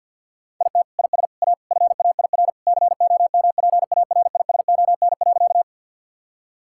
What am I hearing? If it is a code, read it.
Morse code, heard as a keyed tone, about 50 wpm, 700 Hz. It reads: IT IS A FAIR COMPARISON.
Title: Sentences from Top 1600 Words 50wpm